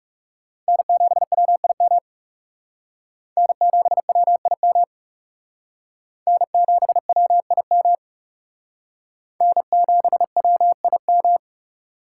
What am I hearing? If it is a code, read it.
Morse code, heard as a keyed tone, about 37 wpm, 700 Hz. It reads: N7WIM N7WIM N7WIM N7WIM
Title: Speed Racing - USA 1x3 Call Signs 30wpm